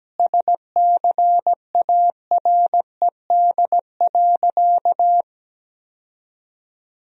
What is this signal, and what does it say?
Morse code, heard as a keyed tone, about 17 wpm, 700 Hz. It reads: SCARED.
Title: Sentences from Top 1000 Words - Review Part 4 - 17wpm